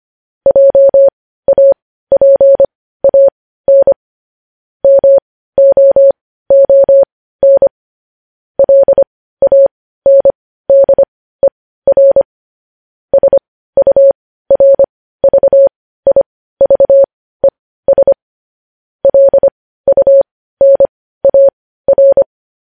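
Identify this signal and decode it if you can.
Morse code, heard as a keyed tone, about 25 wpm, spaced as for 15 wpm, 550 Hz. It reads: JAPAN MOON LANDER SURVIVES LUNAR
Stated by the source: Morse practice sending news headlines at 15 wpm.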